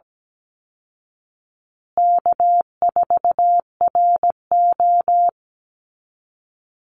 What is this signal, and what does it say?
Morse code, heard as a keyed tone, about 17 wpm, 700 Hz. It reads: K4RO
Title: Easy Call Signs Repeated 2x - 17wpm